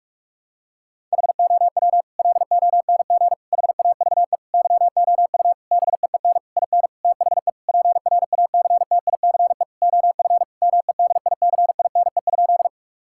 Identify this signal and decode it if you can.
Morse code, heard as a keyed tone, about 46 wpm, 700 Hz. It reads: HOW LONG HAVE YOU BEEN IN THE PRACTICE OF MEDICINE?
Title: Sentences from Top 1000 Words - Review Part 6 - 45wpm